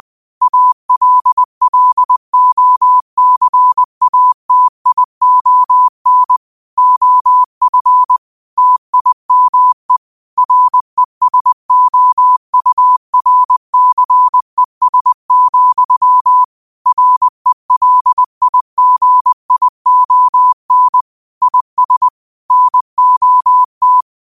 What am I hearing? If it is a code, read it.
Morse code, heard as a keyed tone, about 20 wpm, 1000 Hz. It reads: ALLOCATION OF TIME RESOURCES, RELIGION IS NOT
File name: qotdfiles / 20wpm